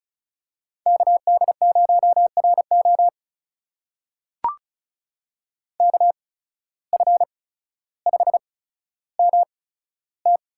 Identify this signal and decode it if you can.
Morse code, heard as a keyed tone, about 36 wpm, spaced as for 15 wpm, 700 Hz. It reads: KD0RO KF5MT
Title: USA Call Signs - 5 Characters - 8x Character Spacing - 35wpm